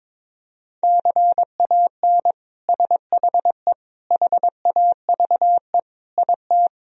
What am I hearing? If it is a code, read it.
Morse code, heard as a keyed tone, about 21 wpm, 700 Hz. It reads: CAN SHE HAVE IT
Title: Sentences from Top 1000 Words - Review Part 4 - 22wpm